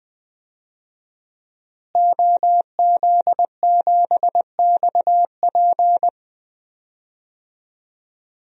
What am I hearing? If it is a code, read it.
Morse code, heard as a keyed tone, about 20 wpm, 700 Hz. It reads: OZ7XP
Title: DXCC-100 Call Signs - 5 Characters - Repeated 2x - 20wpm